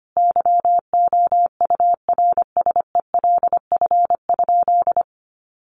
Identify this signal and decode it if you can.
Morse code, heard as a keyed tone, about 25 wpm, 700 Hz. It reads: YOURSELF?
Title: Sentences from Top 1500 Words 25wpm